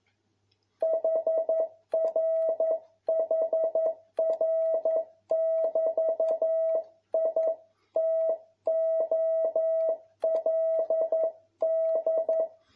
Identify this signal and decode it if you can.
Morse code, heard as a keyed tone, about 11 wpm, 650 Hz. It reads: HRHR<BT>I TOLD
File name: jh8cbh / 2021cw